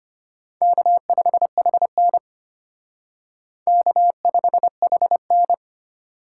Morse code, heard as a keyed tone, about 33 wpm, 700 Hz. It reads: K5HN K5HN